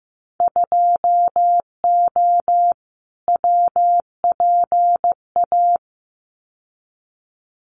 Morse code, heard as a keyed tone, about 15 wpm, 700 Hz. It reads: 2O WPA
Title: Winter Field Day Exchanges Encore 15wpm